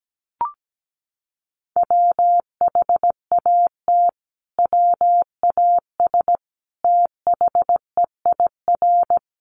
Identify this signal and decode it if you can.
Morse code, heard as a keyed tone, about 17 wpm, 700 Hz. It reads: WHAT WAS THEIR
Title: Sentences from Top 1400 Words 17wpm